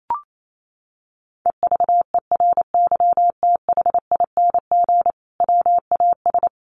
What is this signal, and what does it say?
Morse code, heard as a keyed tone, about 28 wpm, 700 Hz. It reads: EVERYTHING WAS
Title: Sentences from Top 900 Words 28wpm